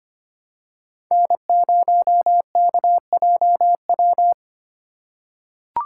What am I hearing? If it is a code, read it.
Morse code, heard as a keyed tone, about 25 wpm, 700 Hz. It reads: N0KJW
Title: USA Call Signs - 5 Characters - 3x Character Spacing - 25wpm